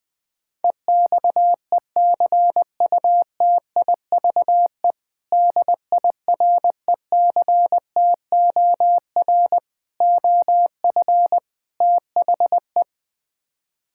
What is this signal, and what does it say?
Morse code, heard as a keyed tone, about 20 wpm, 700 Hz. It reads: EXECUTIVE DIRECTOR OF THE
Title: Build A Phrase - 4-Word Phrases - 20wpm